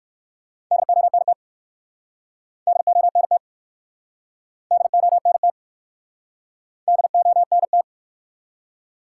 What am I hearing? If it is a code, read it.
Morse code, heard as a keyed tone, about 60 wpm, 700 Hz. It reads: DONT DONT DONT DONT